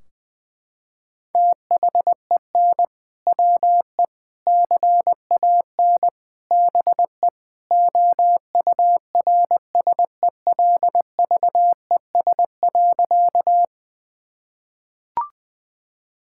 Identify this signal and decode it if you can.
Morse code, heard as a keyed tone, about 20 wpm, 700 Hz. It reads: THEN WE CAN BE OURSELVES.
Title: Sentences from Top 1600 Words 20wpm